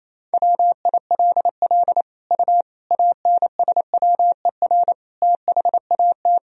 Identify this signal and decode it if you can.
Morse code, heard as a keyed tone, about 28 wpm, 700 Hz. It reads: WILL U ANSWER THAT
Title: Sentences from Top 400 Words and Abbreviations 28wpm